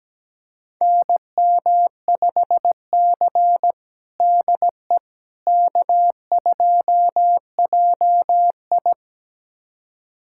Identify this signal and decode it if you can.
Morse code, heard as a keyed tone, about 17 wpm, 700 Hz. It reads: NM5C DE K2JI
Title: USA Callsign DE Callsign - 4 Characters - 17wpm